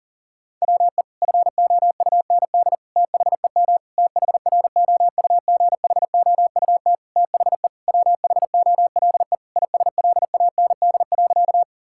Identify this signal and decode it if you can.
Morse code, heard as a keyed tone, about 40 wpm, 700 Hz. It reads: WE FOUND THEM THROUGHOUT THE WHOLE ISLAND.
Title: Sentences from Top 1000 Words - Review Part 5 - 40wpm